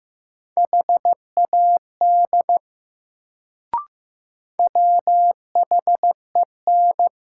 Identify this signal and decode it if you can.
Morse code, heard as a keyed tone, about 15 wpm, 700 Hz. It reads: HAD WHEN
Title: Sets of 2 Words - New Word Drill - 15wpm